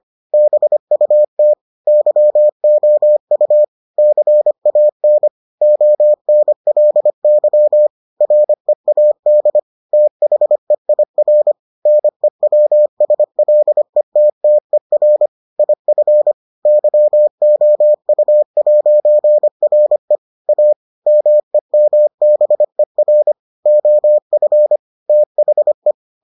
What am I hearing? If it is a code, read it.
Morse code, heard as a keyed tone, about 25 wpm, 600 Hz. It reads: BUT YOU CAN ONLY READ THEIR NEWSLETTER IF YOU'RE A MEMBER OF THE